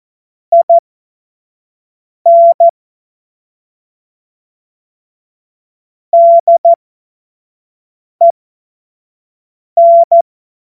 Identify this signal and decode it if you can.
Morse code, heard as a keyed tone, about 14 wpm, spaced as for 5 wpm, 680 Hz. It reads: IN DEN
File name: Morsekurs / Witze / Witz 31. Mai 2022 25 F70